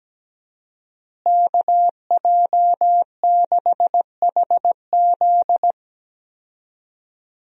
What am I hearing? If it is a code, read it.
Morse code, heard as a keyed tone, about 17 wpm, 700 Hz. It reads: KJ6HZ
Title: USA Call Signs - 5 Characters - Repeated 2x - 17wpm